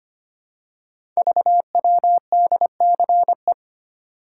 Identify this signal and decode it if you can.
Morse code, heard as a keyed tone, about 25 wpm, 700 Hz. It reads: VWDCE
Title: Code Groups - 5 Letters - 25wpm